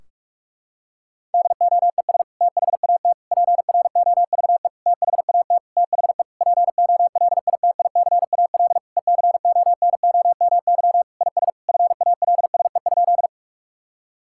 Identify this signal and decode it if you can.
Morse code, heard as a keyed tone, about 41 wpm, 700 Hz. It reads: DOES THAT PROVE THAT THE POLITICAL ECONOMY IS FALSE?
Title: Sentences from Top 1600 Words 45wpm